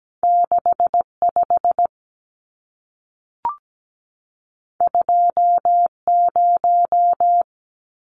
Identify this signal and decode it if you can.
Morse code, heard as a keyed tone, about 17 wpm, 700 Hz. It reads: 65 20
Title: Numbers - 2-Digit 17wpm